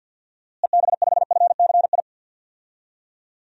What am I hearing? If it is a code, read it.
Morse code, heard as a keyed tone, about 52 wpm, 700 Hz. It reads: EB5FCI
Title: DXCC-100 Call Signs - 6 Characters - Repeated 3x - 50wpm